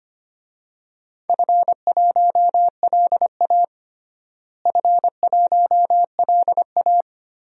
Morse code, heard as a keyed tone, about 25 wpm, 700 Hz. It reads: F1LA F1LA
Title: DXCC-100 Call Signs - 4 Characters - Repeated 3x - 25wpm